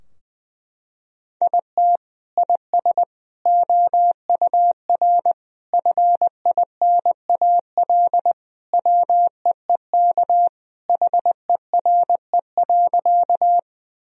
Morse code, heard as a keyed tone, about 21 wpm, 700 Hz. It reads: IT IS OUR FINAL WEEK HERE.
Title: Sentences from Top 1000 Words - Review Part 2 - 20wpm